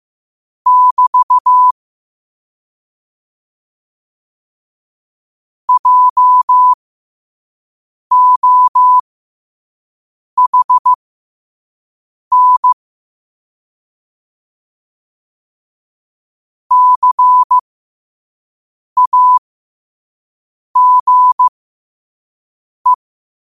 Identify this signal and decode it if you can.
Morse code, heard as a keyed tone, about 15 wpm, spaced as for 5 wpm, 1000 Hz. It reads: <BT> JOHN CAGE
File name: qotdfiles / 5wpm